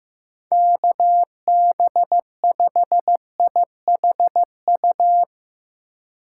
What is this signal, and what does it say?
Morse code, heard as a keyed tone, about 16 wpm, 700 Hz. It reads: KB5IHU
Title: Speed Racing - USA Call Signs 15wpm